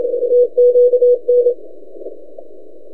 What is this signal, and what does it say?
Morse code, heard as a keyed tone, about 29 wpm, 500 Hz. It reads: TQN
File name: t32rl-20cw